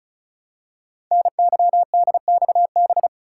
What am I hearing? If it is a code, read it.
Morse code, heard as a keyed tone, about 36 wpm, 700 Hz. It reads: NYDXB
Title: Code Groups - 5 Letters - 35wpm